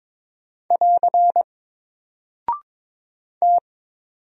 Morse code, heard as a keyed tone, about 22 wpm, 700 Hz. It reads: <AR> T
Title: Single Character (Letter:Number:Punctuation:Prosign) 22wpm